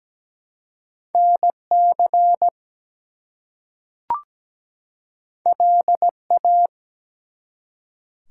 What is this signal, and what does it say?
Morse code, heard as a keyed tone, about 17 wpm, 700 Hz. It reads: NC LA
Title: US States and Canadian Provinces 17wpm